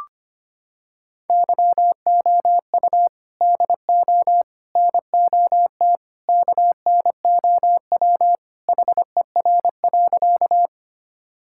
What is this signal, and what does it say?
Morse code, heard as a keyed tone, about 25 wpm, 700 Hz. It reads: YOU DO NOT KNOW HER.
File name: practice / 25wpm / Sentences with 4-Word Phrases - 25wpm